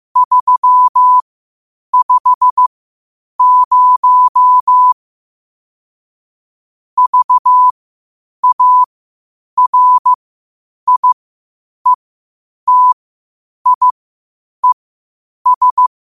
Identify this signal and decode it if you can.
Morse code, heard as a keyed tone, about 15 wpm, spaced as for 8 wpm, 1000 Hz. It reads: 350 VARIETIES
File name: qotdfiles / 8wpm